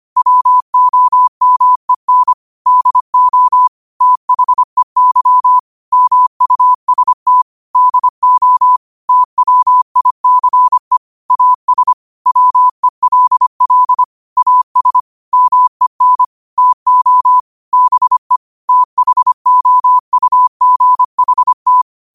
Morse code, heard as a keyed tone, about 25 wpm, 1000 Hz. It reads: WOMEN DO THEY MUST DO TWICE AS WELL AS MEN TO BE THOUGHT